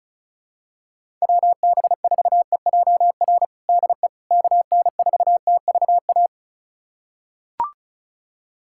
Morse code, heard as a keyed tone, about 34 wpm, 700 Hz. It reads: WB4EJR DE KN4TVA
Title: USA Callsign DE Callsign 35wpm